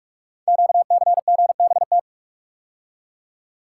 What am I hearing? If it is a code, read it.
Morse code, heard as a keyed tone, about 44 wpm, 700 Hz. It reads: QCGBT
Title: Code Groups - 5 Letter:Numbers - 45wpm